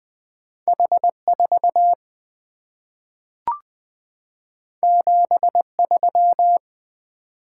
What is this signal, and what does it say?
Morse code, heard as a keyed tone, about 20 wpm, 700 Hz. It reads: H 4 7 3
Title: Two Letter-Number Combinations - 20wpm